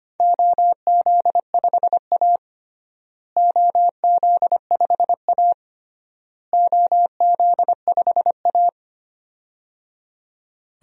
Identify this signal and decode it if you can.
Morse code, heard as a keyed tone, about 25 wpm, 700 Hz. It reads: OZ5A OZ5A OZ5A
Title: DXCC-100 Call Signs - 4 Characters - Repeated 3x - 25wpm